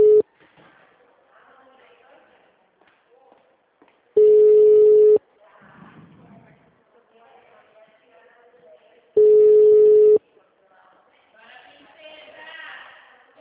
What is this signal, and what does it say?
Ringback tone (Europe, 425 Hz), with speech.